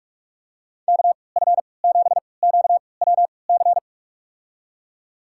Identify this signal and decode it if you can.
Morse code, heard as a keyed tone, about 46 wpm, spaced as for 25 wpm, 700 Hz. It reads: KF7QWC
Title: USA Call Signs - 3x Character Spacing - 45wpm